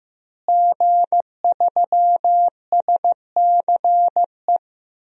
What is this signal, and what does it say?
Morse code, heard as a keyed tone, about 15 wpm, 700 Hz. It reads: G3SCE
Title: DXCC-100 Call Signs - 5 Characters - Repeated 2x - 15wpm